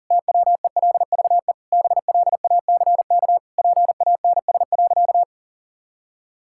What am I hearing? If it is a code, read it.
Morse code, heard as a keyed tone, about 40 wpm, 700 Hz. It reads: TWELVE BLACK PANS.
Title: Pangram Challenge v2 - 40wpm